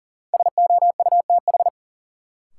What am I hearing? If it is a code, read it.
Morse code, heard as a keyed tone, about 40 wpm, 700 Hz. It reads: SOUTH
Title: Sentences from Top 400 Words - New Word Drill - 40wpm